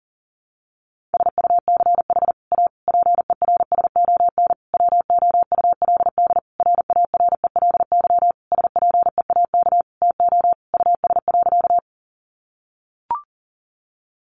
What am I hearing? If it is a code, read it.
Morse code, heard as a keyed tone, about 40 wpm, 700 Hz. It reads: SUCH A PERSON WOULD RARELY SPEAK TO US.
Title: Sentences from Top 1400 Words 40wpm